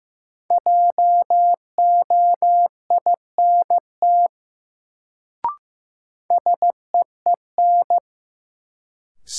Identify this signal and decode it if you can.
Morse code, heard as a keyed tone, about 15 wpm, 700 Hz. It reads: JOINT SEEN
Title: Sets of 5 Words Encore - New Word Drill - 15wpm